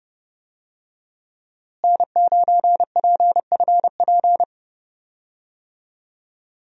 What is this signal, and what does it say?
Morse code, heard as a keyed tone, about 30 wpm, 700 Hz. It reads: N9PFP
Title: USA Call Signs - 5 Characters - Repeated 2x - 30wpm